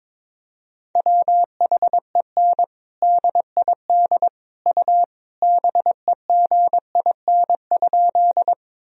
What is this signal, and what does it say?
Morse code, heard as a keyed tone, about 22 wpm, 700 Hz. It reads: WHEN DID U BEGIN?
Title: Sentences from Top 300 Words and Abbreviations 22wpm